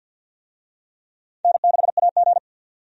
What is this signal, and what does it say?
Morse code, heard as a keyed tone, about 50 wpm, 700 Hz. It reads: N6AG
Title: USA Call Signs - 4 Characters - 50wpm